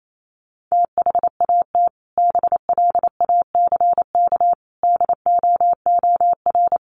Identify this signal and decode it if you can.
Morse code, heard as a keyed tone, about 28 wpm, 700 Hz. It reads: THAT BLACK DOOR
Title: Sentences from Top 1400 Words 28wpm